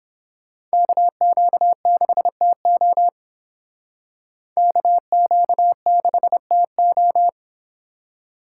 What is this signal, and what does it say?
Morse code, heard as a keyed tone, about 28 wpm, 700 Hz. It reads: KQ6TO KQ6TO